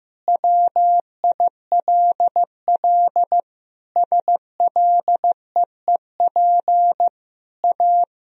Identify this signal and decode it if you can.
Morse code, heard as a keyed tone, about 15 wpm, 700 Hz. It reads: WILL SLEEP A
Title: Sentences from Top 1000 Words - Review Part 2 - 15wpm